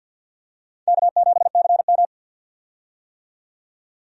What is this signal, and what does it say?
Morse code, heard as a keyed tone, about 46 wpm, 700 Hz. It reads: K7CM
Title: USA Call Signs - 4 Characters - 50wpm